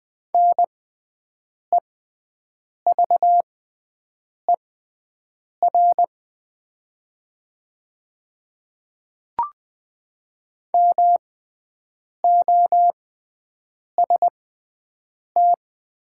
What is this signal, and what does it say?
Morse code, heard as a keyed tone, about 20 wpm, spaced as for 7 wpm, 700 Hz. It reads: NEVER MOST
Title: 6x Character Spacing - Top 300 Words - 20wpm